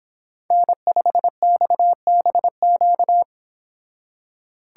Morse code, heard as a keyed tone, about 25 wpm, 700 Hz. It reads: N5XBQ